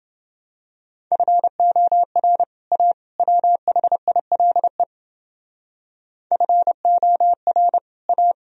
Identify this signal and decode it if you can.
Morse code, heard as a keyed tone, about 29 wpm, 700 Hz. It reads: FOR A WHILE FOR A